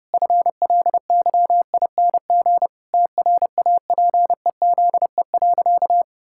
Morse code, heard as a keyed tone, about 30 wpm, 700 Hz. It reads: FLYING TRAPEZE.
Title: Pangram Challenge v2 - 30wpm